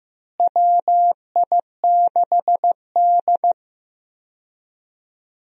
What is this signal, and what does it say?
Morse code, heard as a keyed tone, about 16 wpm, 700 Hz. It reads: WI6D
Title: USA Call Signs - 4 Characters - Repeated 3x - 15wpm